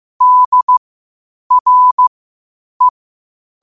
Morse code, heard as a keyed tone, about 15 wpm, spaced as for 8 wpm, 1000 Hz. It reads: DRE
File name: qotdfiles / 8wpm